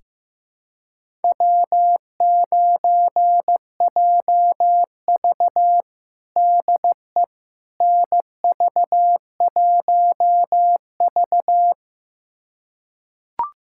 Morse code, heard as a keyed tone, about 15 wpm, 700 Hz. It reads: W9JV DE NV1V
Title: USA Callsign DE Callsign - 4 Characters - 15wpm